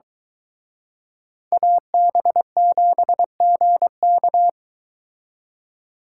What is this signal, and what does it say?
Morse code, heard as a keyed tone, about 23 wpm, 700 Hz. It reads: AB7GK